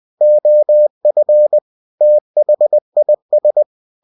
Morse code, heard as a keyed tone, about 20 wpm, 600 Hz. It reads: OF THIS
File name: cw-books / winnie-the-pooh / winnie-20wpm-page-0112